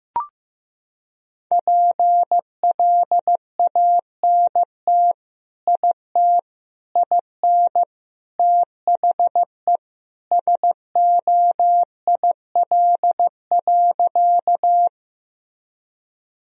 Morse code, heard as a keyed tone, about 15 wpm, 700 Hz. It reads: PLANT IT IN THE SOIL.
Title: Sentences from Top 1000 Words - Review Part 4 - 15wpm